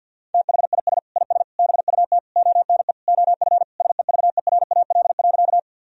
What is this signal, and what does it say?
Morse code, heard as a keyed tone, about 50 wpm, 700 Hz. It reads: THIS IS BUT ONE OF SEVERAL.
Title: Sentences from Top 1000 Words - Review Part 2 - 50wpm